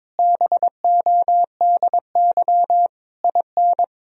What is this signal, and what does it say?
Morse code, heard as a keyed tone, about 22 wpm, 700 Hz. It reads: BODY IN
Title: Sentences from Top 1000 Words - Review Part 4 - 22wpm